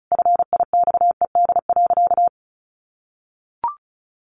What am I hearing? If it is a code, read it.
Morse code, heard as a keyed tone, about 34 wpm, 700 Hz. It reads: FIXED.
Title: Sentences from Top 1100 Words 35wpm